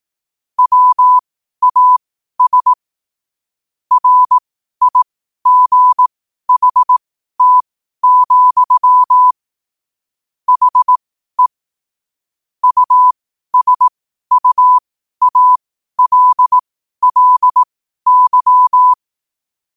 Morse code, heard as a keyed tone, about 18 wpm, spaced as for 12 wpm, 1000 Hz. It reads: WAS RIGHT, HE USUALLY